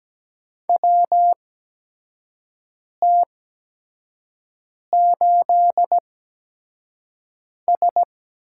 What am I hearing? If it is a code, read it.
Morse code, heard as a keyed tone, about 17 wpm, spaced as for 5 wpm, 700 Hz. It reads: WT8S
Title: USA Call Signs - 4 Characters - 8x Character Spacing - 17wpm